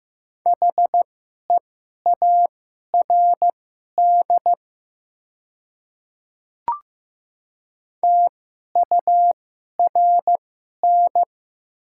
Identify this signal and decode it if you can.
Morse code, heard as a keyed tone, about 15 wpm, spaced as for 11 wpm, 700 Hz. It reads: HEARD TURN
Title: 2x Character Spacing - Top 500 Words - 15wpm